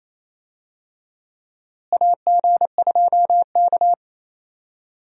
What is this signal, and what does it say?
Morse code, heard as a keyed tone, about 28 wpm, 700 Hz. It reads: AG2K
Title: USA Call Signs - 4 Characters - Repeated 3x - 28wpm